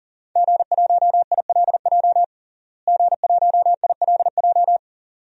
Morse code, heard as a keyed tone, about 40 wpm, 690 Hz. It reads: G1ILJ G1ILJ
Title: DXCC-100 Call Signs - 5 Characters - Repeated 2x - 40wpm